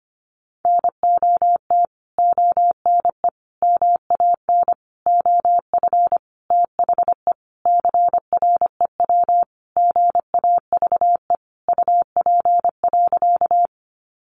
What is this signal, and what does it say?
Morse code, heard as a keyed tone, about 25 wpm, 700 Hz. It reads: NOT ONE MAN OF THE CREW GAVE UP.
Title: Sentences from Top 1500 Words 25wpm